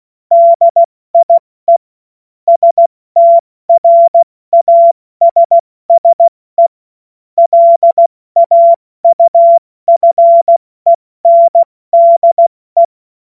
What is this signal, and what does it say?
Morse code, heard as a keyed tone, about 16 wpm, 680 Hz. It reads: DIE STRASSE LAUFENDE
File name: Morsekurs / Witze / Witz 03 Juni Mai 2025 70 F80